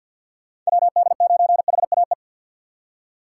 Morse code, heard as a keyed tone, about 65 wpm, 700 Hz. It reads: WD9HAE